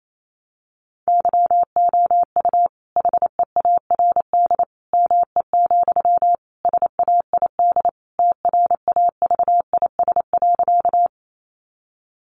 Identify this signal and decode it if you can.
Morse code, heard as a keyed tone, about 28 wpm, 700 Hz. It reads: YOU HEARD ME, SAID TRAVIS.
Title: A Sound Of Thunder 28wpm